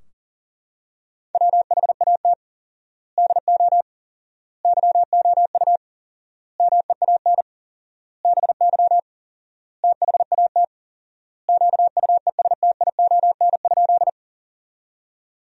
Morse code, heard as a keyed tone, about 40 wpm, 700 Hz. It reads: WHAT DO YOU MEAN BY THAT QUESTION?
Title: 4x Word-Spacing - Sentences from Top 400 Words - 40wpm